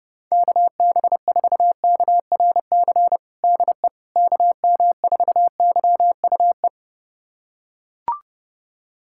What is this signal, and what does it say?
Morse code, heard as a keyed tone, about 30 wpm, 700 Hz. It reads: KB4KRC DE KM4YUE
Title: USA Callsign DE Callsign - 6 Characters - 30wpm